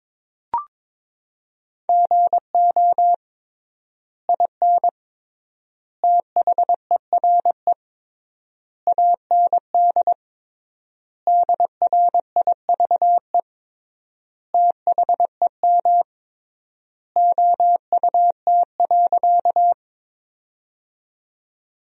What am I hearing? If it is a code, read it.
Morse code, heard as a keyed tone, about 22 wpm, 700 Hz. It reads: GO IN THERE AND DRIVE THEM OUT.
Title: 3x Word-Spacing - Sentences from Top 500 Words - 22wpm